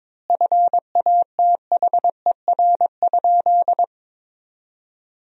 Morse code, heard as a keyed tone, about 22 wpm, 700 Hz. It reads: FATHER?
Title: 3x Word-Spacing - Sentences from Top 400 Words - 22wpm